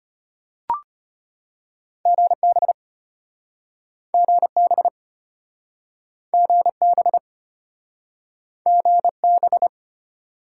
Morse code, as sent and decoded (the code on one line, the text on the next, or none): --. -... / --. -... / --. -... / --. -...
GB GB GB GB